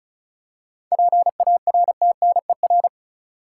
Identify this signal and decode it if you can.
Morse code, heard as a keyed tone, about 34 wpm, 700 Hz. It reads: PARTNER
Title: Seven Letter Words - 35wpm